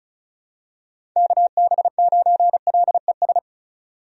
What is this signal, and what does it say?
Morse code, heard as a keyed tone, about 36 wpm, 700 Hz. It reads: KB9LES
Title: USA Call Signs - 6 Characters - 35wpm